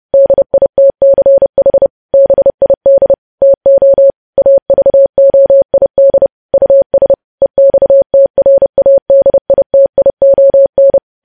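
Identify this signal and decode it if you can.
Morse code, heard as a keyed tone, about 30 wpm, 550 Hz. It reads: DITCH BID TO AVOID US EXTRADITION
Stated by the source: Morse code practice of news headlines sent at 30 wpm.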